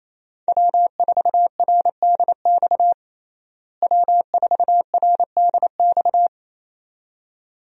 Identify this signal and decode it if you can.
Morse code, heard as a keyed tone, about 27 wpm, 700 Hz. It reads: W4RDX W4RDX
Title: USA Call Signs - 5 Characters - Repeated 2x - 28wpm